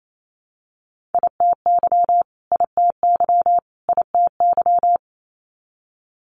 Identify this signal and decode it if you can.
Morse code, heard as a keyed tone, about 28 wpm, 700 Hz. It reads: ITY ITY ITY
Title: Multi-letter Phonemes Repeated 3x 28wpm